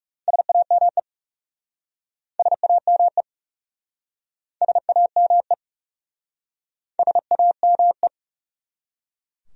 Morse code, heard as a keyed tone, about 36 wpm, 700 Hz. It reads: SAME SAME SAME SAME
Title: Speed-Racing Top 300 Words 30wpm